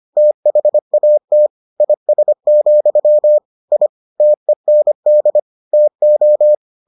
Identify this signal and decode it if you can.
Morse code, heard as a keyed tone, about 25 wpm, 600 Hz. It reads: THAT IS, I TEND TO